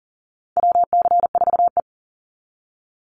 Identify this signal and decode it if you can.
Morse code, heard as a keyed tone, about 40 wpm, 700 Hz. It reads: WC4E
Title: Call Signs Repeated 2x - 40wpm